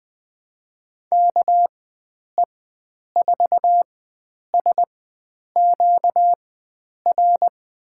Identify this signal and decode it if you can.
Morse code, heard as a keyed tone, about 21 wpm, spaced as for 9 wpm, 700 Hz. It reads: KE4SQR
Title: USA Call Signs - 6 Characters - 4x Character Spacing - 20wpm